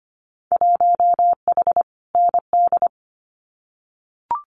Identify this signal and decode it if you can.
Morse code, heard as a keyed tone, about 25 wpm, 700 Hz. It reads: 1H ND
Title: Winter Field Day Exchanges 25wpm